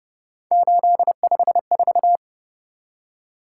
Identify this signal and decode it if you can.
Morse code, heard as a keyed tone, about 29 wpm, 700 Hz. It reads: 854